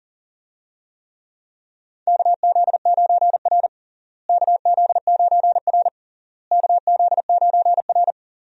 Morse code, heard as a keyed tone, about 37 wpm, 700 Hz. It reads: KZ9R KZ9R KZ9R
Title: USA Call Signs - 4 Characters - Repeated 3x - 40wpm